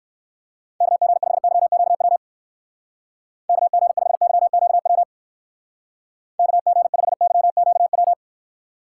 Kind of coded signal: Morse code

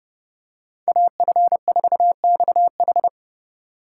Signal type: Morse code

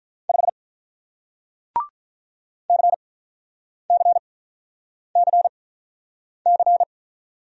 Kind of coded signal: Morse code

